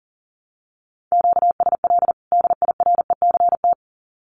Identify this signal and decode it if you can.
Morse code, heard as a keyed tone, about 40 wpm, 700 Hz. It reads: QSL DIRECT
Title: Common QSO Phrases 40wpm